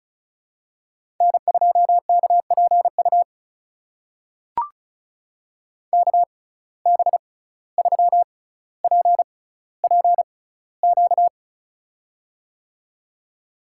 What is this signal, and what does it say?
Morse code, heard as a keyed tone, about 36 wpm, spaced as for 17 wpm, 700 Hz. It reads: N2KPU KB3PPQ
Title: USA Call Signs - 6x Character Spacing - 35wpm